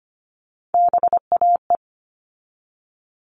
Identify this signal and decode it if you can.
Morse code, heard as a keyed tone, about 25 wpm, 700 Hz. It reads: BAE
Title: All Cut Numbers - 3-Digit 25wpm